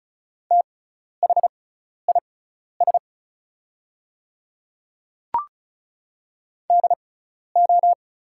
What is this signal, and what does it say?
Morse code, heard as a keyed tone, about 36 wpm, spaced as for 12 wpm, 700 Hz. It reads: THIS DO
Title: 6x Character Spacing - Top 100 Words - 35wpm